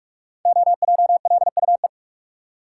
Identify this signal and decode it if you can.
Morse code, heard as a keyed tone, about 44 wpm, 700 Hz. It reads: OJLUE
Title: Code Groups - 5 Letter:Numbers - 45wpm